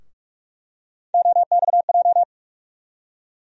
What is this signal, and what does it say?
Morse code, heard as a keyed tone, about 42 wpm, 700 Hz. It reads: OXJ
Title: USA Call Sign Postfixes - 3 Letters 45wpm